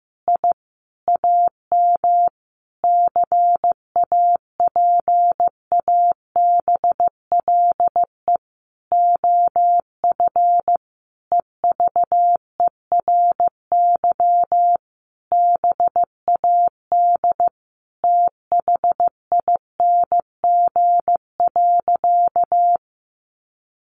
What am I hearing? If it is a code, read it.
Morse code, heard as a keyed tone, about 15 wpm, 700 Hz. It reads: I AM CAPABLE OF EVERY BAD THING.
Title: Sentences from Top 1200 Words 15wpm